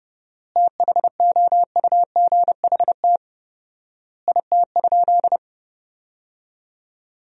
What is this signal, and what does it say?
Morse code, heard as a keyed tone, about 30 wpm, 700 Hz. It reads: THOUGHT IT?
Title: 4x Word-Spacing - Sentences from Top 300 Words - 30wpm